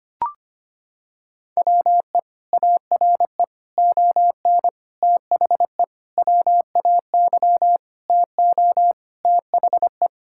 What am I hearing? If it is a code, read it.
Morse code, heard as a keyed tone, about 25 wpm, 700 Hz. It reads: WE ARE ON THE WAY TO THE
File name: practice / 25wpm / Sentences with 4-Word Phrases - 25wpm